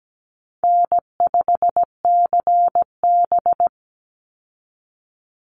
Morse code, heard as a keyed tone, about 17 wpm, 700 Hz. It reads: N5CB
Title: USA Call Signs 17wpm